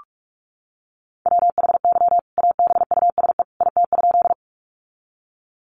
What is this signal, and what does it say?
Morse code, heard as a keyed tone, about 44 wpm, 700 Hz. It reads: WHY ABUSE IT?
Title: Sentences from Top 1300 Words 45wpm